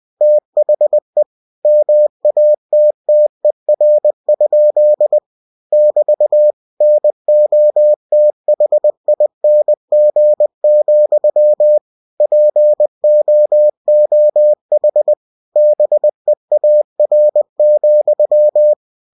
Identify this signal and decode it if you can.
Morse code, heard as a keyed tone, about 20 wpm, 600 Hz. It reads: THE MATTER? <BT> NOTHING, POOH BEAR,
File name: cw-books / winnie-the-pooh / winnie-20wpm-page-0045